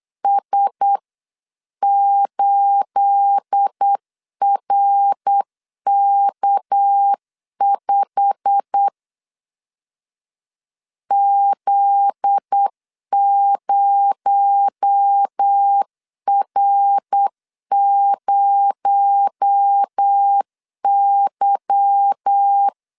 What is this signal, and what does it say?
Morse code, heard as a keyed tone, about 8 wpm, 800 Hz. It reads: S8RK5 Z0R0Y
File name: CW / DET / 1 9 1500C S A 8wpm 5637